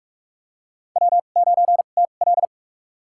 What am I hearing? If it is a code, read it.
Morse code, heard as a keyed tone, about 44 wpm, spaced as for 32 wpm, 700 Hz. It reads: W9TL
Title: USA Call Signs - 4 Characters - 2x Character Spacing - 45wpm